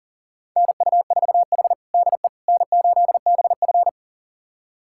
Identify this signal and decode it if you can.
Morse code, heard as a keyed tone, about 40 wpm, 700 Hz. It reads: NU4H DE N8BF
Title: USA Callsign DE Callsign - 4 Characters - 40wpm